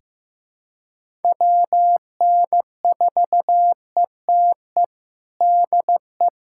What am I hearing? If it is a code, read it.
Morse code, heard as a keyed tone, about 15 wpm, 700 Hz. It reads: WN4ETE DE
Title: USA Callsign DE Callsign 15wpm